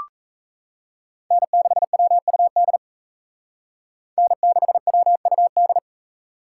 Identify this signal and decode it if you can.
Morse code, heard as a keyed tone, about 34 wpm, 700 Hz. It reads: N6WUD N6WUD
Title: Speed Racing - USA 1x3 Call Signs 28wpm